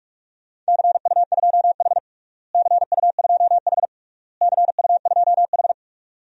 Morse code, heard as a keyed tone, about 43 wpm, 700 Hz. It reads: CU2H CU2H CU2H
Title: DXCC-100 Call Signs - 4 Characters - Repeated 3x - 45wpm